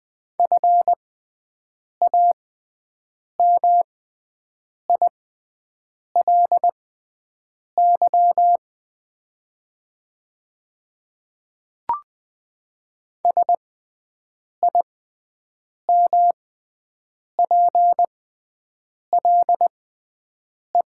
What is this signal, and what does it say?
Morse code, heard as a keyed tone, about 21 wpm, spaced as for 7 wpm, 700 Hz. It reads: FAMILY SIMPLE
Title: 6x Character Spacing - Top 500 Words - 20wpm